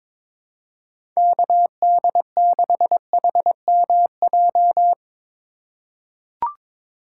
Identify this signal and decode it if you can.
Morse code, heard as a keyed tone, about 22 wpm, 700 Hz. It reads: KD6HMJ
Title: USA Call Signs - 6 Characters - 3x Character Spacing - 22wpm